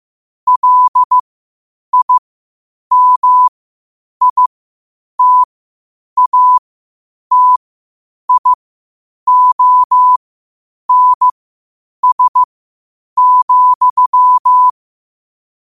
Morse code, heard as a keyed tone, about 15 wpm, spaced as for 8 wpm, 1000 Hz. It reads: LIMITATIONS,
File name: qotdfiles / 8wpm